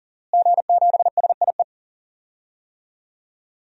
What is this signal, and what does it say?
Morse code, heard as a keyed tone, about 40 wpm, 700 Hz. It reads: G7SIE
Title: DXCC-100 Call Signs - 5 Characters - Repeated 2x - 40wpm